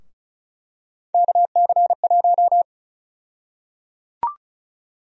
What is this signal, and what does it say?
Morse code, heard as a keyed tone, about 34 wpm, 700 Hz. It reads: KC1
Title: USA Call Sign Prefixes - 35wpm